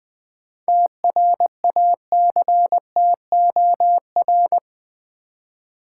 Morse code, heard as a keyed tone, about 20 wpm, 700 Hz. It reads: TRACTOR
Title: Easy Words with Hints - 20wpm